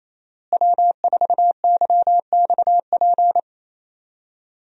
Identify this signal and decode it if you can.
Morse code, heard as a keyed tone, about 28 wpm, 700 Hz. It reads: W4YXP
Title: USA Call Signs - 5 Characters - Repeated 3x - 28wpm